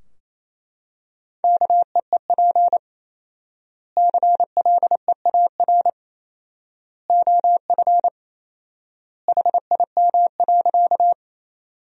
Morse code, heard as a keyed tone, about 28 wpm, 700 Hz. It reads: KEEP CLEAR OF HIM.
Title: 4x Word-Spacing - Sentences from Top 500 Words - 28wpm